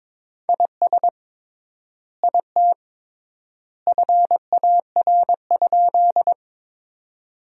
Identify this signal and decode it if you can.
Morse code, heard as a keyed tone, about 22 wpm, 700 Hz. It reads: IS IT FAR?
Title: 3x Word-Spacing - Sentences from Top 300 Words - 22wpm